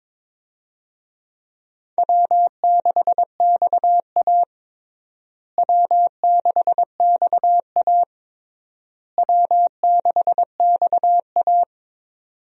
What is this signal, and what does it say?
Morse code, heard as a keyed tone, about 22 wpm, 700 Hz. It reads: W6XA W6XA W6XA
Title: USA Call Signs - 4 Characters - Repeated 3x - 22wpm